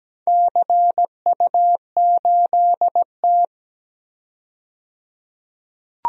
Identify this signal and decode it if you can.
Morse code, heard as a keyed tone, about 17 wpm, 700 Hz. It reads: CU8T
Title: DXCC-100 Call Signs - 4 Characters - 17wpm